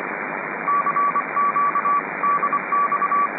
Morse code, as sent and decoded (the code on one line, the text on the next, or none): -.-. --.- / -.. -..-
CQ DX